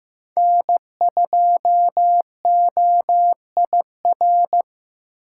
Morse code, heard as a keyed tone, about 15 wpm, 700 Hz. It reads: N2OIR
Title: USA Call Signs - 5 Characters - 2x Character Spacing - 15wpm